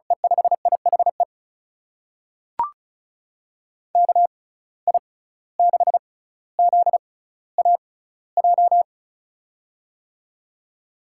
Morse code, heard as a keyed tone, about 36 wpm, spaced as for 17 wpm, 700 Hz. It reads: E5IHE KI6ZAJ
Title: USA Call Signs - 6 Characters - 6x Character Spacing - 35wpm